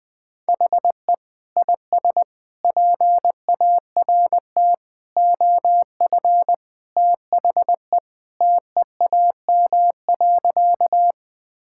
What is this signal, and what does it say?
Morse code, heard as a keyed tone, about 20 wpm, 700 Hz. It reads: HE IS PART OF THE TEAM.
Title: Sentences with 3-Word Phrases - 20wpm